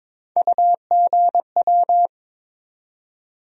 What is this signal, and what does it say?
Morse code, heard as a keyed tone, about 22 wpm, 700 Hz. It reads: UGW
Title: USA Call Sign Postfixes - 3 Letters 22wpm